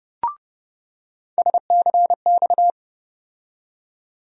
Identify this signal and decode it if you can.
Morse code, heard as a keyed tone, about 32 wpm, 700 Hz. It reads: SCX
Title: USA Call Sign Postfixes - 3 Letters 30wpm